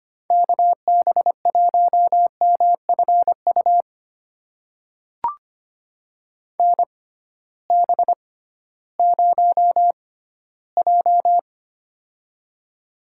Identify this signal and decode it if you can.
Morse code, heard as a keyed tone, about 25 wpm, spaced as for 14 wpm, 700 Hz. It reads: KB1MFU NB0J
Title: USA Call Signs - 6x Character Spacing - 25wpm